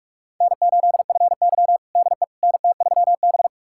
Morse code, heard as a keyed tone, about 41 wpm, 700 Hz. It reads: N8FY DE NT3B
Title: USA Callsign DE Callsign - 4 Characters - 45wpm